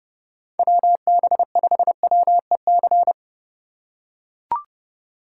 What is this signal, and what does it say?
Morse code, heard as a keyed tone, about 30 wpm, 700 Hz. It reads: WB5WEC
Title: K1USN SST QSO Practice - 30wpm